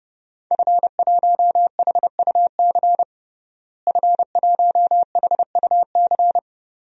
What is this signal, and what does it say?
Morse code, heard as a keyed tone, about 31 wpm, 700 Hz. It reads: F1HUC F1HUC
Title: DXCC-100 Call Signs - 5 Characters - Repeated 3x - 30wpm